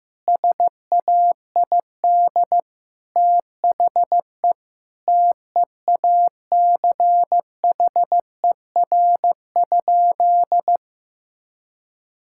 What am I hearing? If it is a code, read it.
Morse code, heard as a keyed tone, about 16 wpm, 700 Hz. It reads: SAID THE TEACHER?
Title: Sentences from Top 1000 Words - Review Part 6 - 15wpm